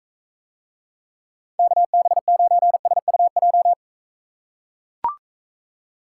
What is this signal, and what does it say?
Morse code, heard as a keyed tone, about 43 wpm, 700 Hz. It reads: KB9SUJ